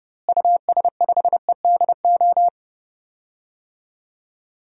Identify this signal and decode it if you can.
Morse code, heard as a keyed tone, about 30 wpm, 700 Hz. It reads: US5EDO
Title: DXCC-100 Call Signs - 6 Characters - Repeated 2x - 30wpm